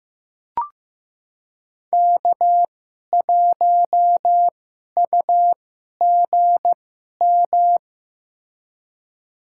Morse code, heard as a keyed tone, about 15 wpm, spaced as for 11 wpm, 700 Hz. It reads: K1UGM
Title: USA Call Signs - 5 Characters - 2x Character Spacing - 15wpm